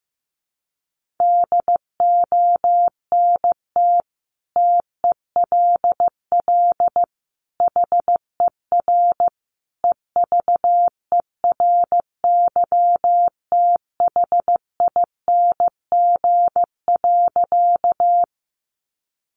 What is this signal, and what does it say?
Morse code, heard as a keyed tone, about 14 wpm, 700 Hz. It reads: DONT TELL HER EVERYTHING.
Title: Sentences from Top 700 Words 15wpm